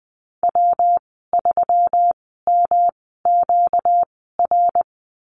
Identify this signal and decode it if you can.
Morse code, heard as a keyed tone, about 20 wpm, spaced as for 14 wpm, 700 Hz. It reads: W3MQR
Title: 238 Lesson 38g - Letter q - Call Signs 20wpm